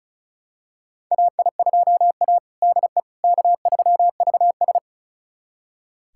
Morse code, heard as a keyed tone, about 34 wpm, 700 Hz. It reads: AI2A DE K3VS
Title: USA Callsign DE Callsign - 4 Characters - 35wpm